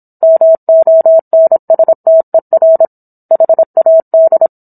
Morse code, heard as a keyed tone, about 27 wpm, 650 Hz. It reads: MONSTER HAD